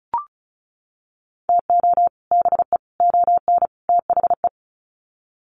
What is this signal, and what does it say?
Morse code, heard as a keyed tone, about 34 wpm, 700 Hz. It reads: TO BE ON THE
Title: Sets of 4 Words 35wpm